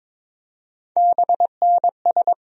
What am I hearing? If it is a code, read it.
Morse code, heard as a keyed tone, about 22 wpm, 700 Hz. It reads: BNS
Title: USA Call Sign Postfixes - 3 Letters 22wpm